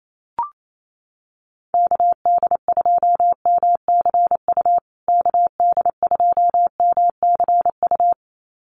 Morse code, heard as a keyed tone, about 28 wpm, 700 Hz. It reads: KD2MCU KD2MCU
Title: USA Call Signs Repeated 2x 28wpm